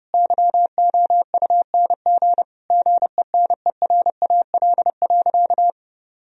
Morse code, heard as a keyed tone, about 30 wpm, 700 Hz. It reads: YOUNG GENERAL.
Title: Sentences from Top 1000 Words - Review Part 3 - 30wpm